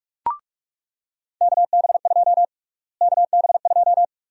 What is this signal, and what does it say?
Morse code, heard as a keyed tone, about 41 wpm, 700 Hz. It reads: KB2 KB2
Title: USA Call Sign Prefixes - Repeated 2x - 45wpm